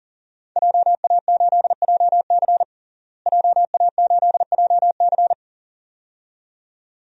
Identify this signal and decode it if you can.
Morse code, heard as a keyed tone, about 40 wpm, 700 Hz. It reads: JA8JC JA8JC
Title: DXCC-100 Call Signs - 5 Characters - Repeated 2x - 40wpm